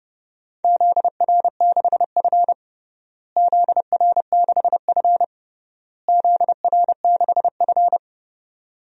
Morse code, heard as a keyed tone, about 30 wpm, 700 Hz. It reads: ZR6F ZR6F ZR6F
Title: DXCC-100 Call Signs - 4 Characters - Repeated 3x - 30wpm